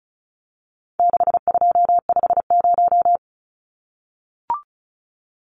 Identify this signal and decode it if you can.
Morse code, heard as a keyed tone, about 34 wpm, 700 Hz. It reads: 6250
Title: Numbers - 4-Digit 35wpm